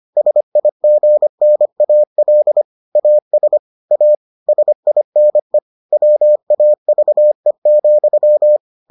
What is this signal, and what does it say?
Morse code, heard as a keyed tone, about 25 wpm, 600 Hz. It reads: SIGNAL AS A SINE WAVE,